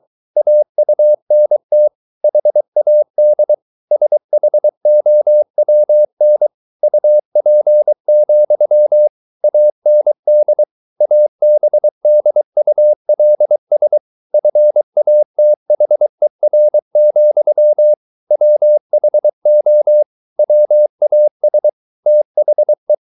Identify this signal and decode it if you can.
Morse code, heard as a keyed tone, about 23 wpm, 600 Hz. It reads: AUNT HAD SHOWN UP, AND ABDULS FATHER, WHO WAS THE